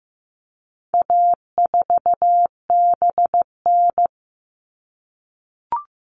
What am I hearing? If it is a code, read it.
Morse code, heard as a keyed tone, about 15 wpm, 700 Hz. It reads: A4BN
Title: All Cut Numbers - 4-Digit 15wpm